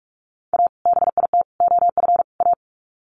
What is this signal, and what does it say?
Morse code, heard as a keyed tone, about 44 wpm, 700 Hz. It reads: A BIT OF A
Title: Sets of 4 Words 45wpm